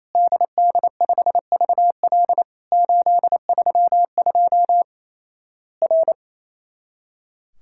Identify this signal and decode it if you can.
Morse code, heard as a keyed tone, about 28 wpm, 700 Hz. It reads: DD5VL 832 R
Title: Worked All Europe DX Contest - QTC Exchange - 28wpm